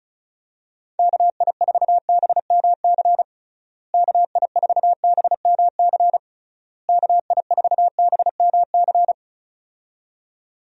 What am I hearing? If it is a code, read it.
Morse code, heard as a keyed tone, about 34 wpm, 700 Hz. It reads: KI4BMC KI4BMC KI4BMC
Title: USA Call Signs - 6 Characters - Repeated 3x - 35wpm